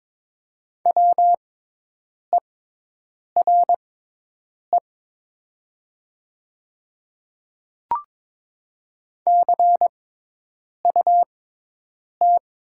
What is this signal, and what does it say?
Morse code, heard as a keyed tone, about 22 wpm, spaced as for 8 wpm, 700 Hz. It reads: WERE CUT
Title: 6x Character Spacing - Top 500 Words - 22wpm